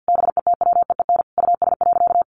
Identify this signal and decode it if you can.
Morse code, heard as a keyed tone, about 50 wpm, 700 Hz. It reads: BETWEEN US.
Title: Sentences from Top 800 Words 50wpm